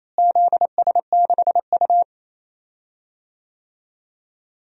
Morse code, heard as a keyed tone, about 28 wpm, 700 Hz. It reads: ZS6U
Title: DXCC-100 Call Signs - 4 Characters - Repeated 3x - 28wpm